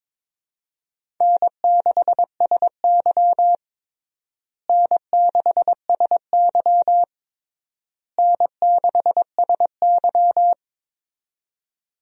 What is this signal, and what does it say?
Morse code, heard as a keyed tone, about 23 wpm, 700 Hz. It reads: N6SY N6SY N6SY
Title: USA Call Signs - 4 Characters - Repeated 3x - 22wpm